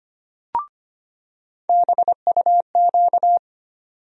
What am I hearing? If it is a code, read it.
Morse code, heard as a keyed tone, about 25 wpm, 700 Hz. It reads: BUQ
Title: USA Call Sign Postfixes - 3 Letters 25wpm